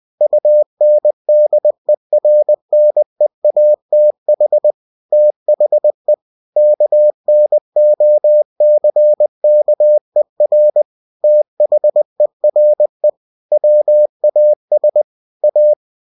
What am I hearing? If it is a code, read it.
Morse code, heard as a keyed tone, about 20 wpm, 600 Hz. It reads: UNDERNEATH THE KNOCKER THERE WAS A
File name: cw-books / winnie-the-pooh / winnie-20wpm-page-0028